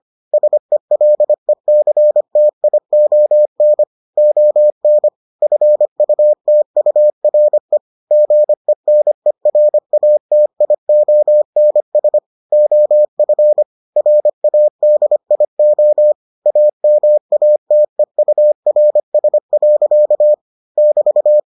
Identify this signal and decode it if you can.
Morse code, heard as a keyed tone, about 25 wpm, 600 Hz. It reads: SELECTION ON FUTURE GENERATIONS OF RADIO AMATEURS. <BT>